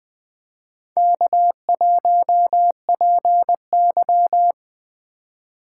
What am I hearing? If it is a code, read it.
Morse code, heard as a keyed tone, about 20 wpm, 700 Hz. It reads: K1PY
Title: USA Call Signs - 4 Characters - 3x Character Spacing - 20wpm